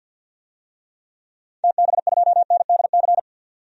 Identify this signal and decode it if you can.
Morse code, heard as a keyed tone, about 52 wpm, 700 Hz. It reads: TB2NDC
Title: DXCC-100 Call Signs - 6 Characters - 50wpm